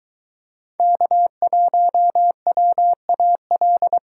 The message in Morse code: K1WAL

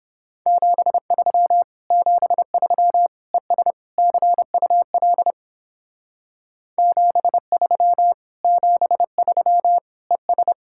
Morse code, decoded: 73 73 ES CUL 73 73 ES

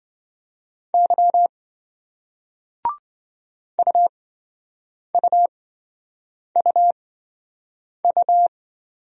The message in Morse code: Y UUUU

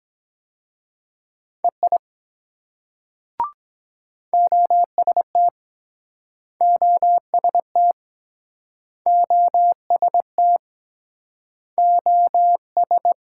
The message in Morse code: EI OST OST OST OS

